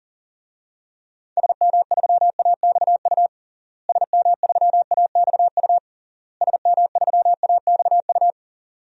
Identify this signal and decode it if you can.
Morse code, heard as a keyed tone, about 40 wpm, 700 Hz. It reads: SM3AXU SM3AXU SM3AXU
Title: DXCC-100 Call Signs - 6 Characters - Repeated 3x - 40wpm